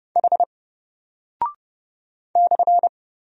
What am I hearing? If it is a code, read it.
Morse code, heard as a keyed tone, about 30 wpm, 700 Hz. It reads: H /
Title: Single Character (Letter:Number:Punctuation:Prosign) 30wpm